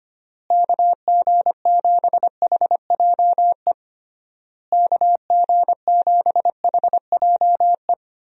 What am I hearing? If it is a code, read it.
Morse code, heard as a keyed tone, about 25 wpm, 700 Hz. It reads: KG7HJE KG7HJE
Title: USA Call Signs - 6 Characters - Repeated 3x - 25wpm